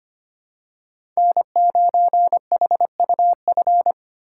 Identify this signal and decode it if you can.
Morse code, heard as a keyed tone, about 25 wpm, 700 Hz. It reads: N9HUF
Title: USA Call Signs - 5 Characters - 3x Character Spacing - 25wpm